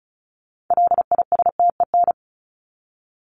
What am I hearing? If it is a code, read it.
Morse code, heard as a keyed tone, about 34 wpm, 700 Hz. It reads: LISTEN
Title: Top 500 Words 35wpm